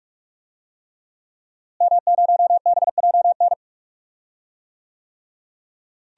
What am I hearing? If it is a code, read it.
Morse code, heard as a keyed tone, about 46 wpm, 700 Hz. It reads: M0BJN
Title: DXCC-100 Call Signs - 5 Characters - Repeated 3x - 45wpm